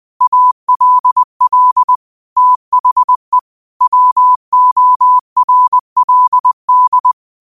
Morse code, heard as a keyed tone, about 20 wpm, 1000 Hz. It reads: ALL THE WORLD
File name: qotdfiles / 20wpm